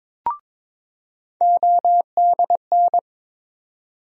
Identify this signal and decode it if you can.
Morse code, heard as a keyed tone, about 22 wpm, 700 Hz. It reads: ODN
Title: USA Call Sign Postfixes - 3 Letters 22wpm